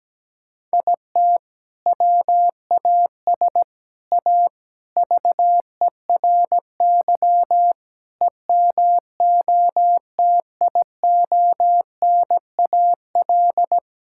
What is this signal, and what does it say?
Morse code, heard as a keyed tone, about 17 wpm, 700 Hz. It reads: IT WAS A VERY EMOTIONAL
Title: Sentences from Top 1000 Words - Review Part 5 - 17wpm